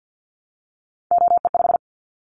Morse code, heard as a keyed tone, about 50 wpm, 700 Hz. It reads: OE5
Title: Callsign Prefixes for Top 50 DXCC - 50wpm